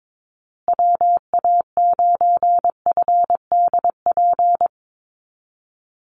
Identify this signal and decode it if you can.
Morse code, heard as a keyed tone, about 22 wpm, 700 Hz. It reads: WA9FDP
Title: USA Call Signs Repeated 2x 22wpm